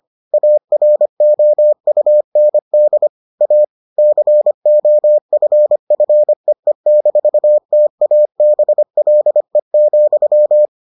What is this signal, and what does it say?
Morse code, heard as a keyed tone, about 25 wpm, 600 Hz. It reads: AROUND A COFFEE-TABLE,